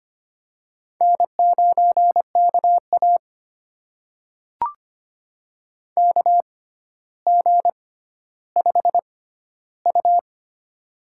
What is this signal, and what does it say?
Morse code, heard as a keyed tone, about 25 wpm, spaced as for 13 wpm, 700 Hz. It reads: N9KA KG5U
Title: USA Call Signs - 4 Characters - 6x Character Spacing - 25wpm